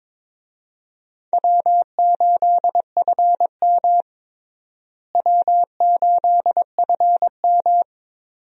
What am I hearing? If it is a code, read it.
Morse code, heard as a keyed tone, about 22 wpm, 700 Hz. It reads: W8FM W8FM
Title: USA Call Signs - 4 Characters - Repeated 2x - 22wpm